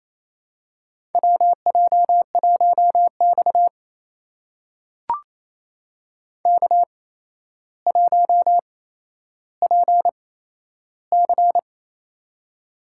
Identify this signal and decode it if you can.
Morse code, heard as a keyed tone, about 28 wpm, spaced as for 12 wpm, 700 Hz. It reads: WJ1X K1PC
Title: USA Call Signs - 4 Characters - 8x Character Spacing - 28wpm